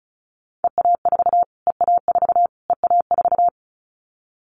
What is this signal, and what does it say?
Morse code, heard as a keyed tone, about 34 wpm, 700 Hz. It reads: EA4 EA4 EA4
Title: Callsign Prefixes for Top 25 DXCC - Repeated 3x - 35wpm